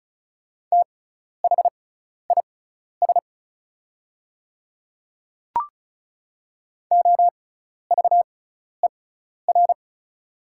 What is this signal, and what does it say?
Morse code, heard as a keyed tone, about 36 wpm, spaced as for 12 wpm, 690 Hz. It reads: THIS OVER
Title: 6x Character Spacing - Top 100 Words - 35wpm